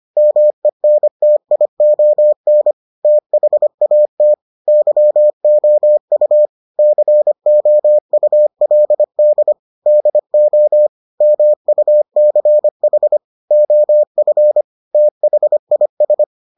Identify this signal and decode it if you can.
Morse code, heard as a keyed tone, about 25 wpm, 600 Hz. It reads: MENTION THAT YOU COULD DO MUCH OF THIS